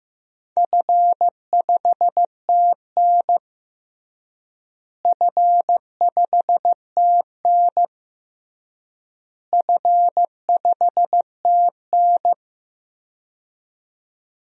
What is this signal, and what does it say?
Morse code, heard as a keyed tone, about 15 wpm, 700 Hz. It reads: F5TN F5TN F5TN
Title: DXCC-100 Call Signs - 4 Characters - Repeated 3x - 15wpm